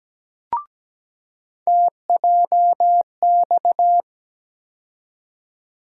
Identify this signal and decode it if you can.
Morse code, heard as a keyed tone, about 17 wpm, 700 Hz. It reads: TJX
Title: USA Call Sign Postfixes - 3 Letters 17wpm